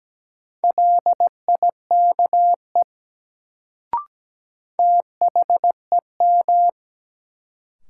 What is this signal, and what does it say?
Morse code, heard as a keyed tone, about 17 wpm, 700 Hz. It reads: LIKE THEM
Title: Sentences from Top 100 Words - New Word Drill - 17wpm